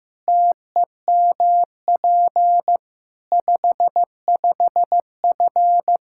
Morse code, heard as a keyed tone, about 15 wpm, 700 Hz. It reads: TEMP 55F
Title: K1USN SST QSO Practice 15wpm